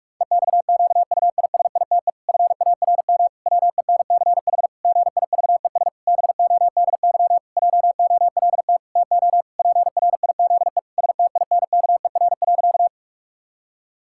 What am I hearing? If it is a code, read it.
Morse code, heard as a keyed tone, about 46 wpm, 700 Hz. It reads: EXQUISITE FARM WENCH GIVES BODY JOLT TO PRIZE STINKER.
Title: Pangram Challenge v2 - 45wpm